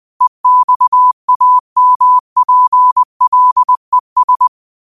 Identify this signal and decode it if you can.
Morse code, heard as a keyed tone, about 20 wpm, 1000 Hz. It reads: EXAMPLES
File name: qotdfiles / 20wpm